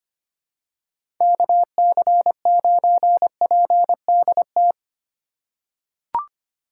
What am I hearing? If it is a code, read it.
Morse code, heard as a keyed tone, about 25 wpm, 700 Hz. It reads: KC9PDT
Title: USA Call Signs - 3x Character Spacing - 25wpm